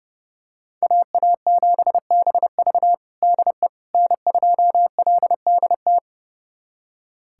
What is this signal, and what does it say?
Morse code, heard as a keyed tone, about 30 wpm, 700 Hz. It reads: AA7BV DE N2LDT
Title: USA Callsign DE Callsign - 5 Characters - 30wpm